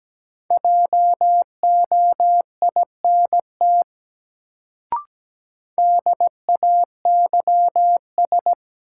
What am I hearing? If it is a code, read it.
Morse code, heard as a keyed tone, about 17 wpm, 700 Hz. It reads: JOINT DAYS
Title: Sets of 5 Words Encore - New Word Drill - 17wpm